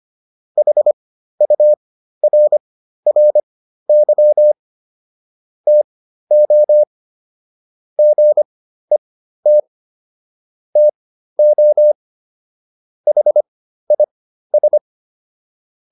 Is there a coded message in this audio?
Yes: Morse code from 0.6 to 14.8 s, heard as a keyed tone, about 25 wpm, spaced as for 13 wpm, 600 Hz. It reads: HURRY TO GET TO HIS